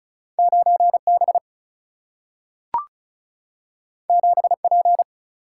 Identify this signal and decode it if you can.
Morse code, heard as a keyed tone, about 36 wpm, 700 Hz. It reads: 9B 7P
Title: Single-Letter Single-Number 35wpm